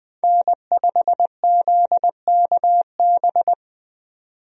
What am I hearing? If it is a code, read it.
Morse code, heard as a keyed tone, about 20 wpm, 700 Hz. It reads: N5ZKB
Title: USA Call Signs - 5 Characters - 20wpm